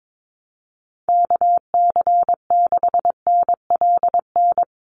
Morse code, heard as a keyed tone, about 22 wpm, 700 Hz. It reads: KC6NLN
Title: USA Call Signs Repeated 3x 22wpm